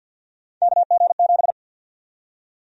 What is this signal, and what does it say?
Morse code, heard as a keyed tone, about 52 wpm, 700 Hz. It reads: KG7
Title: USA Call Sign Prefixes - 50wpm